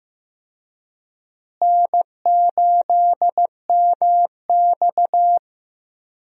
Morse code, heard as a keyed tone, about 15 wpm, 700 Hz. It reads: N8MX
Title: USA Call Signs - 4 Characters - Repeated 2x - 15wpm